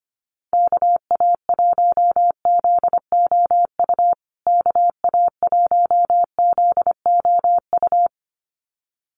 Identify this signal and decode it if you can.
Morse code, heard as a keyed tone, about 25 wpm, 700 Hz. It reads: KA1ZOU KA1ZOU
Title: USA Call Signs Repeated 2x 25wpm